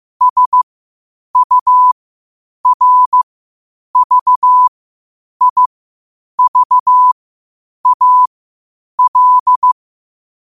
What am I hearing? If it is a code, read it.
Morse code, heard as a keyed tone, about 15 wpm, spaced as for 8 wpm, 1000 Hz. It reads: SURVIVAL
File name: qotdfiles / 8wpm